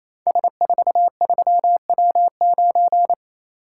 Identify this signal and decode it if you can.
Morse code, heard as a keyed tone, about 27 wpm, 690 Hz. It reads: S43W9
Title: Code Groups - 5 Letter:Numbers - 28wpm